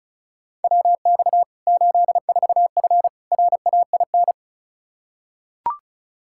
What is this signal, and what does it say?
Morse code, heard as a keyed tone, about 34 wpm, 700 Hz. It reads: WX 84F RAIN
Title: K1USN SST QSO Practice 35wpm